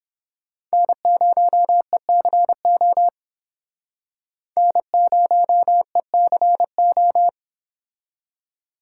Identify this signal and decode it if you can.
Morse code, heard as a keyed tone, about 29 wpm, 700 Hz. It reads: N0ECO N0ECO